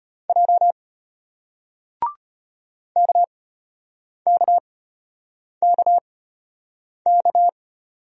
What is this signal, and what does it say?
Morse code, heard as a keyed tone, about 33 wpm, spaced as for 8 wpm, 700 Hz. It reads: J KKKK